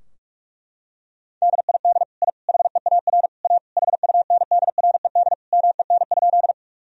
Morse code, heard as a keyed tone, about 46 wpm, 700 Hz. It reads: DID I HEAR A HUNDRED MEN?
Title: Sentences from Top 1000 Words - Review Part 2 - 45wpm